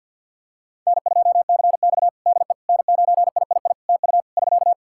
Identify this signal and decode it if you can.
Morse code, heard as a keyed tone, about 52 wpm, 700 Hz. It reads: N2CX DE N9III TU <SK>
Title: K1USN SST QSO Practice 50wpm